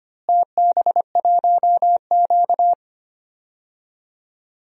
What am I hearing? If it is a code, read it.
Morse code, heard as a keyed tone, about 25 wpm, 700 Hz. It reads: TB1Q
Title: DXCC-100 Call Signs - 4 Characters - Repeated 3x - 25wpm